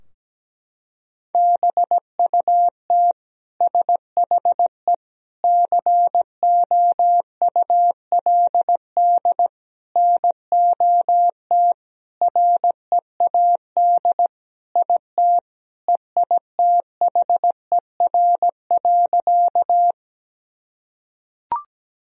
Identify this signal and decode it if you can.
Morse code, heard as a keyed tone, about 17 wpm, 700 Hz. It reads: BUT SHE COULD NOT READ IT EITHER.
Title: Sentences from Top 1000 Words - Review Part 4 - 17wpm